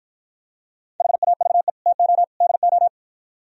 Morse code, heard as a keyed tone, about 52 wpm, 700 Hz. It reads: HAVE TO DO